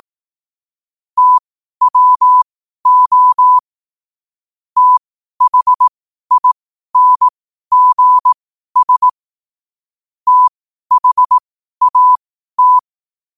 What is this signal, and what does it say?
Morse code, heard as a keyed tone, about 18 wpm, spaced as for 12 wpm, 1000 Hz. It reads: TWO THINGS THAT
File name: qotdfiles / 12wpm